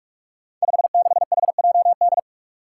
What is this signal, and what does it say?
Morse code, heard as a keyed tone, about 46 wpm, 700 Hz. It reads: 56HJD
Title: Five Challenging Letter-Number Combinations 45wpm